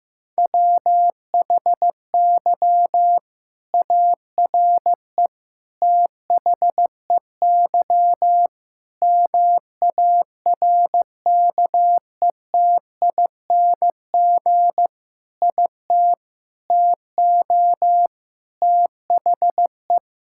Morse code, heard as a keyed tone, about 15 wpm, 700 Hz. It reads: WHY ARE THEY MARKETING IT TO THE
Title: Sentences from Top 1000 Words - Review Part 5 - 15wpm